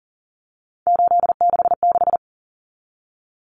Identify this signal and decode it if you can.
Morse code, heard as a keyed tone, about 40 wpm, 700 Hz. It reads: 866
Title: Common Cut Numbers - 3-Digit 40wpm